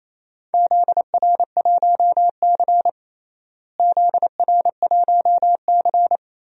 Morse code, heard as a keyed tone, about 28 wpm, 700 Hz. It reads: ZR1C ZR1C
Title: DXCC-100 Call Signs - 4 Characters - Repeated 3x - 28wpm